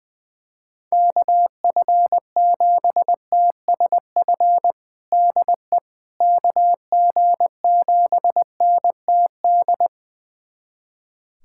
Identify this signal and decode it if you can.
Morse code, heard as a keyed tone, about 21 wpm, 700 Hz. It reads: KF7TSF DE KG7NTD
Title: USA Callsign DE Callsign - 6 Characters - 20wpm